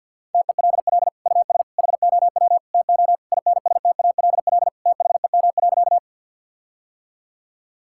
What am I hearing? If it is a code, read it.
Morse code, heard as a keyed tone, about 47 wpm, 700 Hz. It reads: TELL US HOW TO INSTALL THEM.
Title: Sentences from Top 1600 Words 50wpm